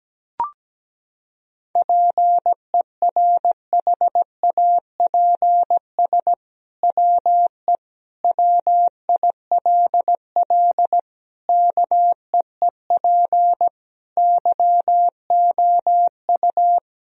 Morse code, heard as a keyed tone, about 17 wpm, 700 Hz. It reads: PERHAPS WE WILL KEEP YOU
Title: Sentences from Top 1000 Words - Review Part 3 - 17wpm